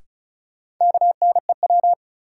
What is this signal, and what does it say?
Morse code, heard as a keyed tone, about 34 wpm, 700 Hz. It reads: KNEW
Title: Sentences from Top 400 Words - New Word Drill - 35wpm